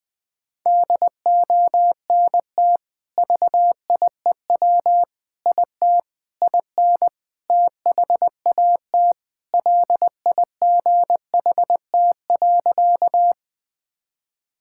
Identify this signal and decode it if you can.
Morse code, heard as a keyed tone, about 20 wpm, 700 Hz. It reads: DONT VIEW IT IN THAT LIGHT.
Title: Sentences from Top 1000 Words - Review Part 6 - 20wpm